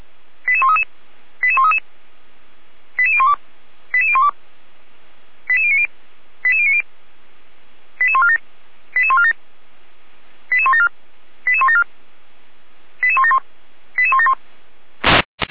Five-tone selective call (ZVEI), heard as quick runs of five tones, tones of about 70 ms. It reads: 88133 88133 88012 88012 88090 88090 88147 88147 88174 88174 88171 88171